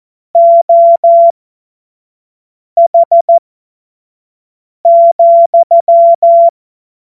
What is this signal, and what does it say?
Morse code, heard as a keyed tone, about 14 wpm, spaced as for 5 wpm, 680 Hz. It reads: OH,